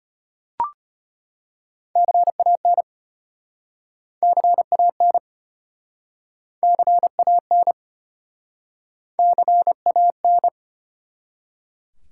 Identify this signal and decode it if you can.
Morse code, heard as a keyed tone, about 34 wpm, 700 Hz. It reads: CAN CAN CAN CAN